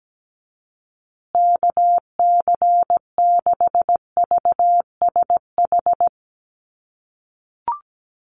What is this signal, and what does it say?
Morse code, heard as a keyed tone, about 17 wpm, 700 Hz. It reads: KC6VSH
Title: USA Call Signs Repeated 2x 17wpm